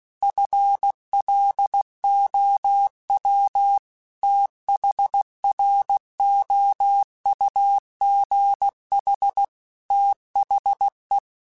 Morse code, heard as a keyed tone, about 16 wpm, 800 Hz. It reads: FLOW THROUGH THE
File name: Technical Subjects / Technical Subjects-mp3s / Reactance @15